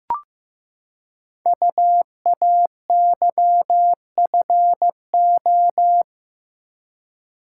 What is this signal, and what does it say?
Morse code, heard as a keyed tone, about 15 wpm, 700 Hz. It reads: UAYFO
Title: Code Groups - 5 Letters - 15wpm